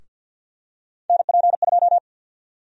Morse code, heard as a keyed tone, about 50 wpm, 700 Hz. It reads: NP2
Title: USA Call Sign Prefixes - 50wpm